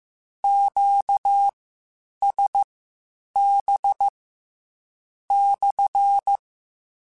Morse code, heard as a keyed tone, about 15 wpm, spaced as for 8 wpm, 800 Hz. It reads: QSB /
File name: Q Calls / Q Calls-mp3s / Q CALLS SPSL @13